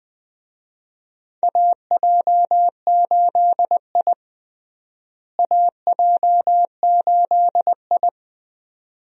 Morse code, heard as a keyed tone, about 20 wpm, 700 Hz. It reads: AJ8I AJ8I
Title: USA Call Signs - 4 Characters - Repeated 2x - 20wpm